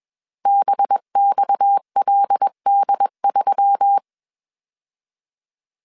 Morse code, heard as a keyed tone, about 21 wpm, 800 Hz. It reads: BXLD3